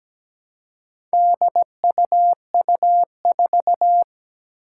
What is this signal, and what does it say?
Morse code, heard as a keyed tone, about 17 wpm, 700 Hz. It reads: DUU4
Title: Four Challenging Letter-Number Combinations 17wpm